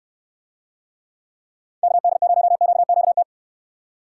Morse code, heard as a keyed tone, about 69 wpm, 700 Hz. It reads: KD0QQT